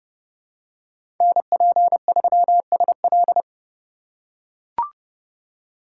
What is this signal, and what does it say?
Morse code, heard as a keyed tone, about 29 wpm, 700 Hz. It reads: NP3SL
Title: USA Call Signs - 5 Characters - 2x Character Spacing - 30wpm